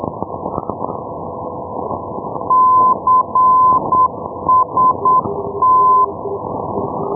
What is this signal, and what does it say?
Morse code, heard as a keyed tone, about 8 wpm, 1020 Hz, with noise. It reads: CST
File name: europe / e-297 ndb palma-costix cst-351+f-2078 ndb perpignan pl-351-1305-day-menorca-rb